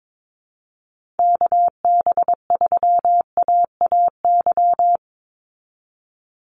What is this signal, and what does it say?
Morse code, heard as a keyed tone, about 22 wpm, 700 Hz. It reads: KB3AAY
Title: Call Signs Repeated 2x - 22wpm